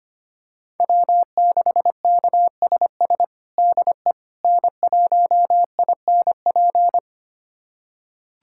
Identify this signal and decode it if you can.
Morse code, heard as a keyed tone, about 25 wpm, 700 Hz. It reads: W6KSS DE N1INP
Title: USA Callsign DE Callsign - 5 Characters - 25wpm